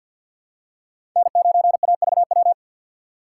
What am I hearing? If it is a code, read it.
Morse code, heard as a keyed tone, about 53 wpm, 700 Hz. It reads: N9AVW
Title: USA Call Signs - 5 Characters - 6x Character Spacing - 50wpm